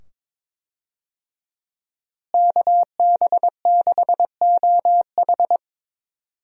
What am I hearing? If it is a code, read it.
Morse code, heard as a keyed tone, about 22 wpm, 700 Hz. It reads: KB6OH
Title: USA Call Signs - 5 Characters - Repeated 2x - 22wpm